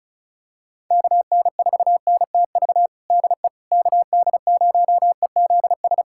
Morse code, heard as a keyed tone, about 34 wpm, 700 Hz. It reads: KN4NTV DE KD0EZS
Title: USA Callsign DE Callsign 35wpm